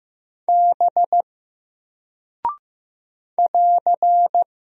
Morse code, heard as a keyed tone, about 15 wpm, 700 Hz. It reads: B <AR>
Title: Single Character (Letter:Number:Punctuation:Prosign) 15wpm